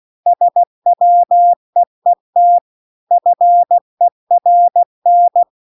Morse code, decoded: SWEET FERN

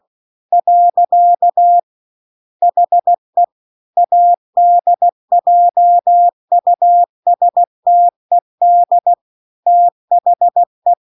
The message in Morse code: . HE ADJUSTED THE

